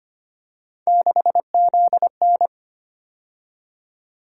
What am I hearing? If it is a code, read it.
Morse code, heard as a keyed tone, about 25 wpm, 700 Hz. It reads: 6ZN
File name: Three Letter-Number Combinations - 25wpm